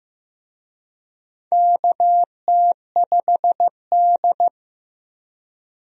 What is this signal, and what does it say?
Morse code, heard as a keyed tone, about 15 wpm, 700 Hz. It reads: KT5D
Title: USA Call Signs - 4 Characters - 15wpm